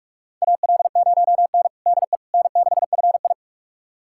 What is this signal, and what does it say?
Morse code, heard as a keyed tone, about 41 wpm, 700 Hz. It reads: AL0N DE N6FI
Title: USA Callsign DE Callsign - 4 Characters - 45wpm